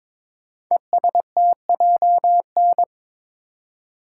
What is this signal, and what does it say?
Morse code, heard as a keyed tone, about 22 wpm, 700 Hz. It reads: ESTJN
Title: Code Groups - 5 Letter:Numbers - 22wpm